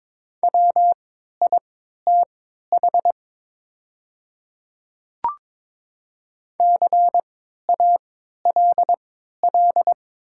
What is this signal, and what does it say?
Morse code, heard as a keyed tone, about 22 wpm, spaced as for 12 wpm, 700 Hz. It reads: WITH CALL
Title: 3x Character Spacing - Top 100 Words - 22wpm